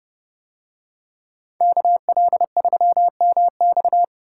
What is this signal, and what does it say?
Morse code, heard as a keyed tone, about 30 wpm, 700 Hz. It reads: KL3MX